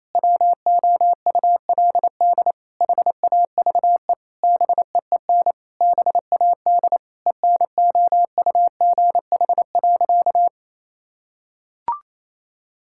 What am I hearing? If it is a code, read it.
Morse code, heard as a keyed tone, about 27 wpm, 700 Hz. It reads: WOULD HAVE BEEN BAD ENOUGH.
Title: Sentences from Top 1000 Words - Review Part 5 - 28wpm